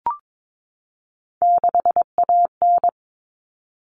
Morse code, heard as a keyed tone, about 22 wpm, 700 Hz. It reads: 6AN
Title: Common Cut Numbers - 3-Digit 22wpm